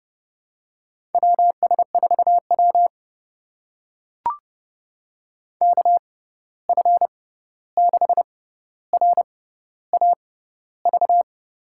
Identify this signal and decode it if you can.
Morse code, heard as a keyed tone, about 30 wpm, spaced as for 14 wpm, 700 Hz. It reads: WS4W KF6RAV
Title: USA Call Signs - 6x Character Spacing - 30wpm